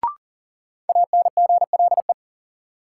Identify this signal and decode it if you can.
Morse code, heard as a keyed tone, about 40 wpm, 700 Hz. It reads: ANGLE
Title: Sentences from Top 1500 Words - New Word Drill Alt - 40wpm